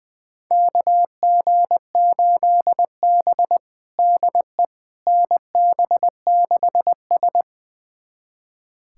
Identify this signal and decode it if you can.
Morse code, heard as a keyed tone, about 20 wpm, 700 Hz. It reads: KG8B DE NB6S
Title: USA Callsign DE Callsign - 4 Characters - 20wpm